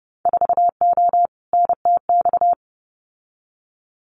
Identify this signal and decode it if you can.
Morse code, heard as a keyed tone, about 30 wpm, 700 Hz. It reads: 4O NTX
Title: Winter Field Day Exchanges 30wpm